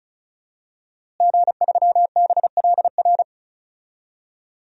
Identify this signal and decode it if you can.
Morse code, heard as a keyed tone, about 34 wpm, 700 Hz. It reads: G3BLR
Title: DXCC-100 Call Signs - 5 Characters - Repeated 3x - 35wpm